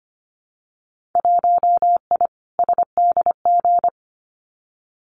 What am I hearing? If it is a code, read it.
Morse code, heard as a keyed tone, about 25 wpm, 700 Hz. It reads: 1I SDG
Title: Winter Field Day Exchanges 25wpm